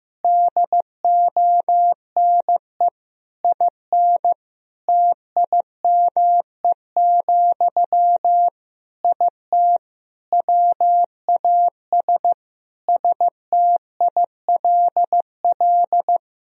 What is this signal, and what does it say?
Morse code, heard as a keyed tone, about 16 wpm, 700 Hz. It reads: DONE IN TIME, IT WAS STILL
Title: Sentences from Top 1000 Words - Review Part 3 - 15wpm